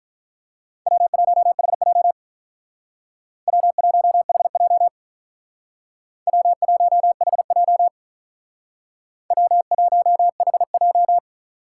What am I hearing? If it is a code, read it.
Morse code, heard as a keyed tone, about 47 wpm, 700 Hz. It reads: W1HJ W1HJ W1HJ W1HJ